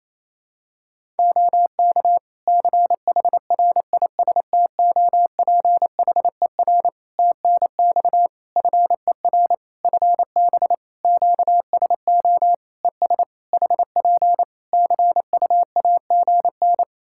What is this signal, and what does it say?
Morse code, heard as a keyed tone, about 28 wpm, 700 Hz. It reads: OK CHRISTOPHER TNX FER FB QSO ES HP CUAGN
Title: Practice QSO - Exchange 4 - Ending - 28wpm